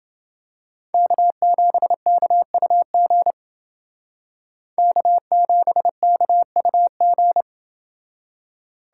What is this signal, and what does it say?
Morse code, heard as a keyed tone, about 30 wpm, 700 Hz. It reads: K7KUG K7KUG